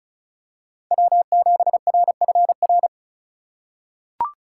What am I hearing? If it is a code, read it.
Morse code, heard as a keyed tone, about 34 wpm, 700 Hz. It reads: W7RFR
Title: USA Call Signs - 5 Characters - 4x Character Spacing - 35wpm